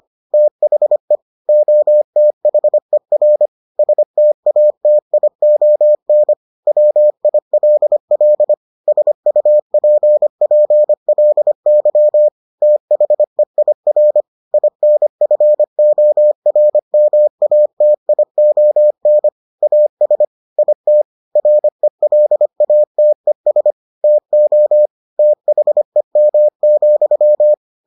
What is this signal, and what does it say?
Morse code, heard as a keyed tone, about 25 wpm, 600 Hz. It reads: THE OTHER STATION WILL SUPPLY THEIR INFORMATION AS IT RELATES TO THEM,